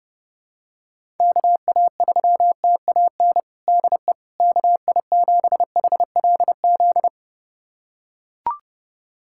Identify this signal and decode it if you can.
Morse code, heard as a keyed tone, about 30 wpm, 700 Hz. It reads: KA3TAN DE KI7HLZ
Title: USA Callsign DE Callsign - 6 Characters - 30wpm